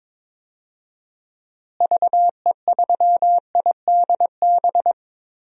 Morse code, heard as a keyed tone, about 22 wpm, 700 Hz. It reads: VE3IDB
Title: DXCC-100 Call Signs - 6 Characters - Repeated 2x - 22wpm